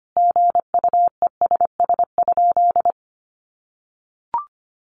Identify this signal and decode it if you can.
Morse code, heard as a keyed tone, about 25 wpm, 700 Hz. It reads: GUESS?
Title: Sentences from Top 1500 Words 25wpm